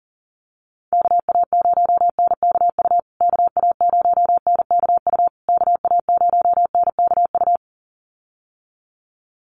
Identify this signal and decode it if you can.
Morse code, heard as a keyed tone, about 40 wpm, 700 Hz. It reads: KA0NKU KA0NKU KA0NKU
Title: USA Call Signs Repeated 3x 40wpm